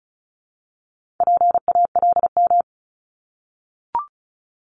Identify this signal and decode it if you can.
Morse code, heard as a keyed tone, about 36 wpm, 700 Hz. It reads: PALM
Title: Multi-letter Phonemes 35wpm